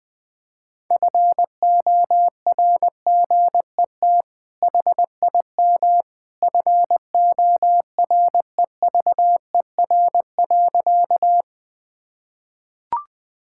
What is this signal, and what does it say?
Morse code, heard as a keyed tone, about 20 wpm, 700 Hz. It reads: FORGET HIM FOREVER.
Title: Sentences from Top 1600 Words 20wpm